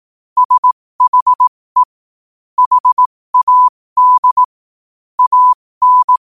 Morse code, heard as a keyed tone, about 18 wpm, spaced as for 15 wpm, 1000 Hz. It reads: SHE HAD AN
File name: qotdfiles / 15wpm